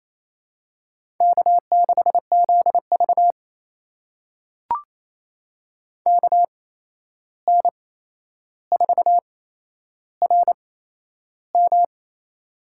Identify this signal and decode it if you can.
Morse code, heard as a keyed tone, about 28 wpm, spaced as for 11 wpm, 700 Hz. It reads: K6ZV KN4RM
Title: USA Call Signs - 8x Character Spacing - 28wpm